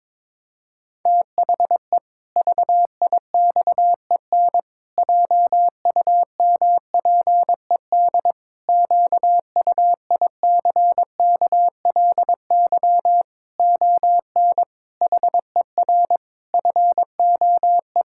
Morse code, heard as a keyed tone, about 22 wpm, 700 Hz. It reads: THE VIXEN JUMPED QUICKLY ON HER FOE
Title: Pangram Challenge v2 - 22wpm